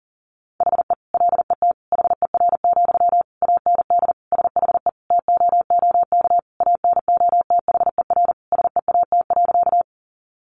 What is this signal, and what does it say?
Morse code, heard as a keyed tone, about 40 wpm, 700 Hz. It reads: HE LET HER, AND SHE TOOK ANOTHER SEAT.
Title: Sentences from Top 1500 Words 40wpm